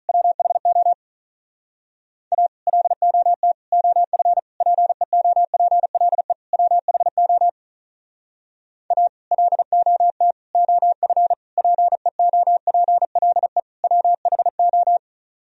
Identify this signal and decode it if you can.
Morse code, heard as a keyed tone, about 45 wpm, 700 Hz. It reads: WHO A LOT OF PEOPLE WHO A LOT OF PEOPLE WHO